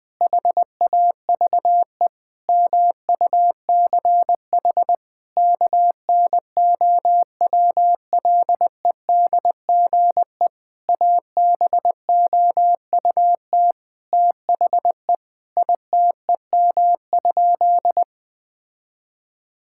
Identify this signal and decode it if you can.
Morse code, heard as a keyed tone, about 20 wpm, 700 Hz. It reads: HAVE MUCH KNOWLEDGE ABOUT THE ITEM?
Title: Sentences from Top 1000 Words - Review Part 6 - 20wpm